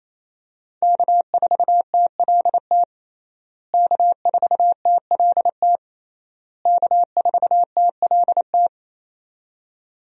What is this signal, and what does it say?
Morse code, heard as a keyed tone, about 28 wpm, 700 Hz. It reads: K4TLT K4TLT K4TLT
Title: USA Call Signs - 5 Characters - Repeated 3x - 28wpm